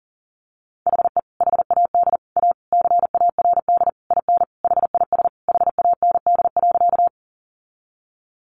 Morse code, heard as a keyed tone, about 40 wpm, 700 Hz. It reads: HE HAD A CARD IN HIS HAND.
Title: Sentences from Top 600 Words 40wpm